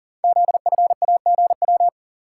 Morse code, heard as a keyed tone, about 40 wpm, 690 Hz. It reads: ZFAGW
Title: Code Groups - 5 Letters - 40wpm